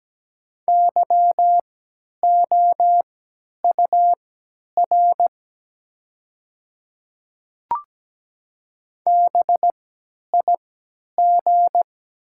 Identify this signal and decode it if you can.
Morse code, heard as a keyed tone, about 17 wpm, spaced as for 10 wpm, 700 Hz. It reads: YOUR BIG
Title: 3x Character Spacing - Top 200 Words - 17wpm